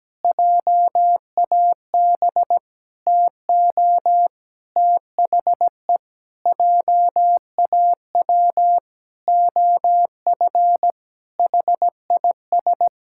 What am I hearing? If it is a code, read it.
Morse code, heard as a keyed tone, about 17 wpm, 700 Hz. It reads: JAB TO THE JAW OF HIS
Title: Pangram Challenge v2 - 17wpm